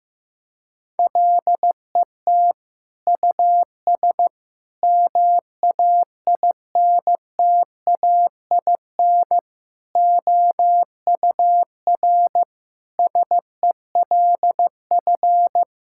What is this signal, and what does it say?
Morse code, heard as a keyed tone, about 15 wpm, 700 Hz. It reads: LET US MAINTAIN OUR SELF
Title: Sentences from Top 1000 Words - Review Part 5 - 15wpm